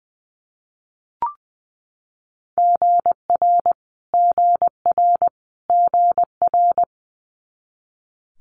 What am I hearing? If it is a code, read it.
Morse code, heard as a keyed tone, about 20 wpm, 700 Hz. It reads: GR GR GR
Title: Multi-letter Phonemes Repeated 3x 20wpm